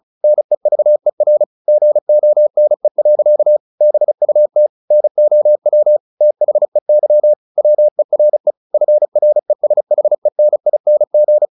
Morse code, heard as a keyed tone, about 34 wpm, 600 Hz. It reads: NEVER GONE. BUT NOW THEY WERE FRESHENING
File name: the old man and the sea / the old man and the sea 1 35wpm